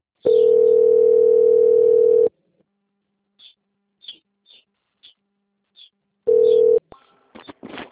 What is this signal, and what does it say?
Ringback tone (North America), 440 + 480 Hz.
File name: out- - - - - .24237